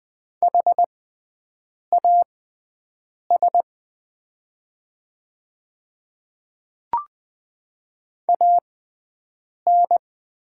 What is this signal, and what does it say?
Morse code, heard as a keyed tone, about 20 wpm, spaced as for 7 wpm, 700 Hz. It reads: HAS AN
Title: 6x Character Spacing - Top 200 Words - 20wpm